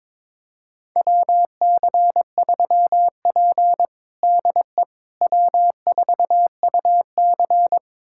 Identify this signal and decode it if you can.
Morse code, heard as a keyed tone, about 22 wpm, 700 Hz. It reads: WC3P DE W4UC
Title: USA Callsign DE Callsign - 4 Characters - 22wpm